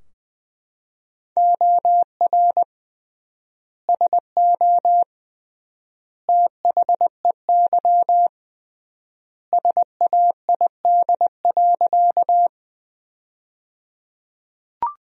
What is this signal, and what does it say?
Morse code, heard as a keyed tone, about 20 wpm, 700 Hz. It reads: OR SO THEY SAID.
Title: 3x Word-Spacing - Sentences from Top 100 Words - 20wpm